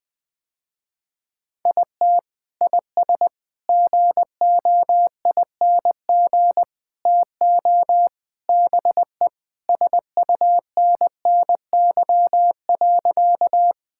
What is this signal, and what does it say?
Morse code, heard as a keyed tone, about 20 wpm, 700 Hz. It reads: IT IS GOING TO BE SUNNY.
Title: Sentences with 4-Word Phrases - Spoken Beginning - 20wpm